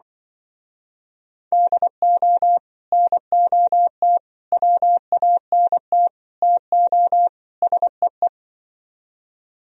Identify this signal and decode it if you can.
Morse code, heard as a keyed tone, about 25 wpm, 700 Hz. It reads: DO NOT WANT TO SEE
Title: Speed-Racing - Sets of 5 Words 20wpm